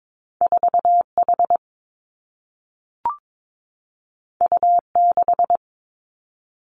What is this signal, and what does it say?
Morse code, heard as a keyed tone, about 22 wpm, 700 Hz. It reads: 4 H U 6
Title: Two Challenging Letter-Number Combinations 22wpm